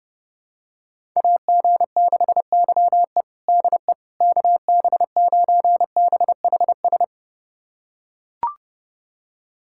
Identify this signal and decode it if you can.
Morse code, heard as a keyed tone, about 30 wpm, 700 Hz. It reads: AG6YE DE KB9BHS
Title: USA Callsign DE Callsign 30wpm